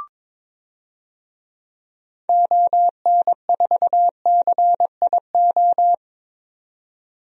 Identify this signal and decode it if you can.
Morse code, heard as a keyed tone, about 22 wpm, 700 Hz. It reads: ON4CIO
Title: DXCC-100 Call Signs - 6 Characters - 22wpm